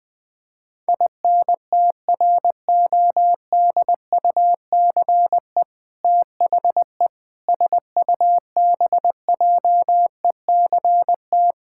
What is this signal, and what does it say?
Morse code, heard as a keyed tone, about 21 wpm, 700 Hz. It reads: INTRODUCE THE SUBJECT
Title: Sentences from Top 1600 Words 20wpm